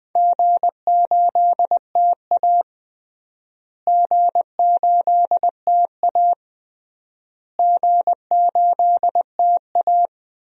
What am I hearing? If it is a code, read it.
Morse code, heard as a keyed tone, about 20 wpm, 700 Hz. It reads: G8TA G8TA G8TA
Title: DXCC-100 Call Signs - 4 Characters - Repeated 3x - 20wpm